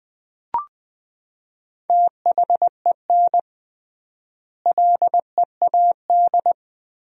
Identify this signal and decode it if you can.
Morse code, heard as a keyed tone, about 21 wpm, 700 Hz. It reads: THEN LEAD
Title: 3x Word-Spacing - Sentences from Top 500 Words - 20wpm